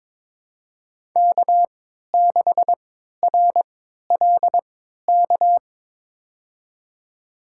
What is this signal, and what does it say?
Morse code, heard as a keyed tone, about 22 wpm, spaced as for 12 wpm, 700 Hz. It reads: K6RLK
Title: USA Call Signs - 5 Characters - 3x Character Spacing - 22wpm